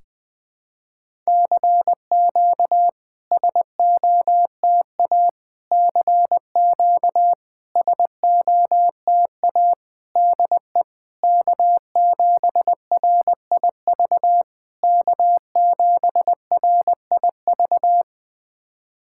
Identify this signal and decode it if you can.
Morse code, heard as a keyed tone, about 21 wpm, 700 Hz. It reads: CQ SOTA CQ SOTA DE K7RIV K7RIV
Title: SOTA QSO Practice 20wpm